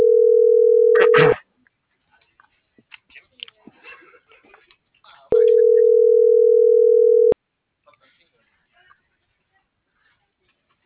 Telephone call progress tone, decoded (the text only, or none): ringback tone